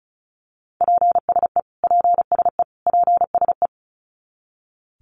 Morse code, heard as a keyed tone, about 34 wpm, 700 Hz. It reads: PSE PSE PSE
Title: Abbreviations Repeated 3x - 35wpm